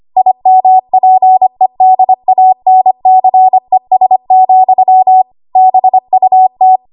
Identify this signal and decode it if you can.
Morse code, heard as a keyed tone, about 25 wpm, 750 Hz. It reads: IMPEDANCES, BUT
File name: Archive / 25 WPM / 25WPM